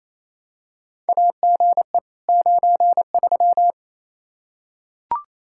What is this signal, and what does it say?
Morse code, heard as a keyed tone, about 28 wpm, 700 Hz. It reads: AGE 93
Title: K1USN SST QSO Practice 28wpm